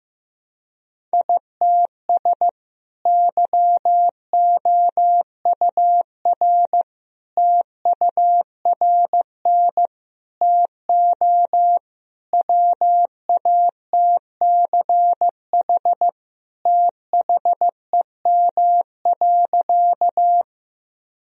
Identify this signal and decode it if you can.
Morse code, heard as a keyed tone, about 15 wpm, 700 Hz. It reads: ITS YOUR TURN TO WATCH THEM.
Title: Sentences from Top 1000 Words - Review Part 1 - 15wpm